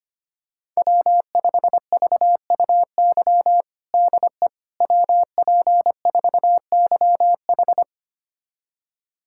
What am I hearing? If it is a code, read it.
Morse code, heard as a keyed tone, about 25 wpm, 700 Hz. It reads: W5VUY DE WP4YH
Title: USA Callsign DE Callsign - 5 Characters - 25wpm